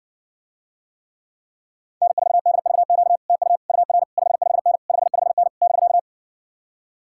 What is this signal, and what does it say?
Morse code, heard as a keyed tone, about 65 wpm, 700 Hz. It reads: N4DVQ TU UR 55N 55N <BK>